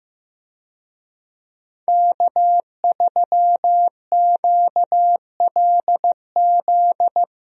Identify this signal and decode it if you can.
Morse code, heard as a keyed tone, about 15 wpm, 700 Hz. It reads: K3QLZ
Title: USA Call Signs - 5 Characters - Repeated 2x - 15wpm